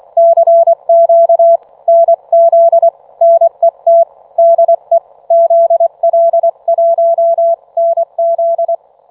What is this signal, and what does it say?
Morse code, heard as a keyed tone, about 24 wpm, 670 Hz. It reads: CQ NZ NET DE ZL1NZ